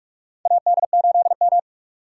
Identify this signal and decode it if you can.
Morse code, heard as a keyed tone, about 46 wpm, 700 Hz. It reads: AD8M
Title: USA Call Signs - 4 Characters - 6x Character Spacing - 45wpm